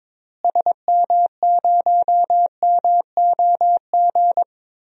Morse code, heard as a keyed tone, about 22 wpm, 700 Hz. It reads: SM0MOG
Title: DXCC-100 Call Signs - 6 Characters - Repeated 2x - 22wpm